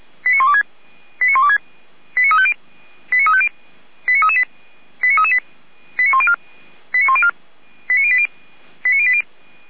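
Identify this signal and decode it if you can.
Five-tone selective call (ZVEI), heard as quick runs of five tones, tones of about 70 ms. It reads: 89126 89126 89350 89350 89338 89338 89104 89104 89080 89080
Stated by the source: Five-tone ZVEI selective call received by SDR, from feuerwehr-orlamuende.org audio archive (fire brigade Orlamünde).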